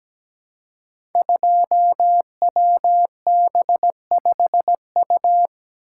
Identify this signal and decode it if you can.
Morse code, heard as a keyed tone, about 17 wpm, 700 Hz. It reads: 2WB5U
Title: Code Groups - 5 Letter:Numbers - 17wpm